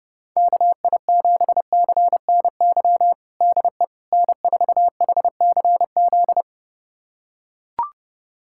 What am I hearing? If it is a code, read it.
Morse code, heard as a keyed tone, about 32 wpm, 690 Hz. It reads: KI7CNY DE N4HCZ
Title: USA Callsign DE Callsign 30wpm